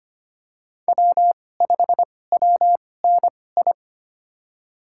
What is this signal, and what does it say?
Morse code, heard as a keyed tone, about 25 wpm, spaced as for 18 wpm, 700 Hz. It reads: W5WNI
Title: USA Call Signs - 5 Characters - 2x Character Spacing - 25wpm